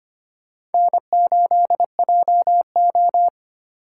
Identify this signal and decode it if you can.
Morse code, heard as a keyed tone, about 31 wpm, 700 Hz. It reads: N8JO
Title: Speed Racing - USA 1x2 Call Signs 25wpm